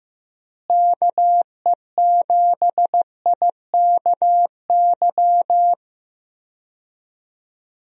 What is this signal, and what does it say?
Morse code, heard as a keyed tone, about 15 wpm, 700 Hz. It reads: KE7IKY
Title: USA Call Signs - 6 Characters - Repeated 2x - 15wpm